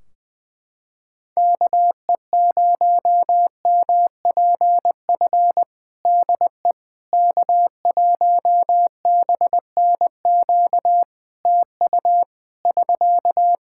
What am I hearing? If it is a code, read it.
Morse code, heard as a keyed tone, about 21 wpm, 700 Hz. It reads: KE0MPF DE K1BNQ TU <SK>
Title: Practice QSO - Exchange 4 - Ending - 20wpm